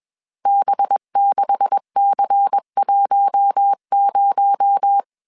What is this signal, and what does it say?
Morse code, heard as a keyed tone, about 21 wpm, 800 Hz. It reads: B6C10